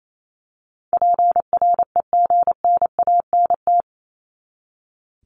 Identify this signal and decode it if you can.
Morse code, heard as a keyed tone, about 28 wpm, 700 Hz. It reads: PREGNANT
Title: Top 751-1000 Words 28wpm